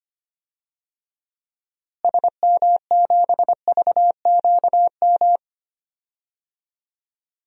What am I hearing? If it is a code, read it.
Morse code, heard as a keyed tone, about 25 wpm, 700 Hz. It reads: SM7VQM
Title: DXCC-100 Call Signs - 6 Characters - Repeated 2x - 25wpm